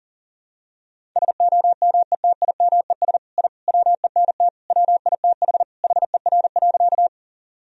Morse code, heard as a keyed tone, about 40 wpm, 700 Hz. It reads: SOMETIMES I WENT WITH HER.
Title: Sentences from Top 1000 Words - Review Part 3 - 40wpm